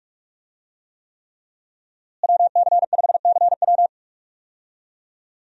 Morse code, heard as a keyed tone, about 43 wpm, 700 Hz. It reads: WC5CW
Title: USA Call Signs - 5 Characters - Repeated 3x - 45wpm